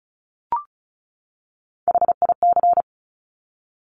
Morse code, heard as a keyed tone, about 36 wpm, 700 Hz. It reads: HIC
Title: Top Three Letter Combinations in Words 35wpm